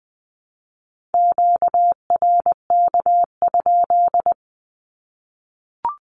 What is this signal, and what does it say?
Morse code, heard as a keyed tone, about 20 wpm, 700 Hz. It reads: QRK?
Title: Q-Codes 20wpm